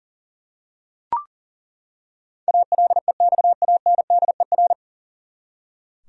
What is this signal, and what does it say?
Morse code, heard as a keyed tone, about 40 wpm, 700 Hz. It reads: ALEXANDER
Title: USA Common Names - 40wpm